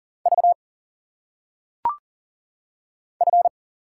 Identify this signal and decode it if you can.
Morse code, heard as a keyed tone, about 40 wpm, 700 Hz. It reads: V F
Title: Single Letters - Phonetic Pronunciation 40wpm